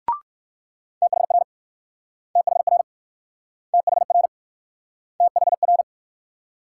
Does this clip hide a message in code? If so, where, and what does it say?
Yes: Morse code from 1.0 to 5.8 s, heard as a keyed tone, about 61 wpm, 700 Hz. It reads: THR THR THR THR